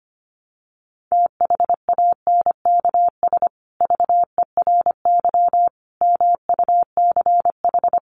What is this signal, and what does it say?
Morse code, heard as a keyed tone, about 25 wpm, 700 Hz. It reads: THANKS VERY MUCH
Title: Sets of 5 Words Encore2 25wpm